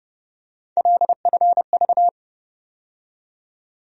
Morse code, heard as a keyed tone, about 30 wpm, 700 Hz. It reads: LFV
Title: USA Call Sign Postfixes - 3 Letters 30wpm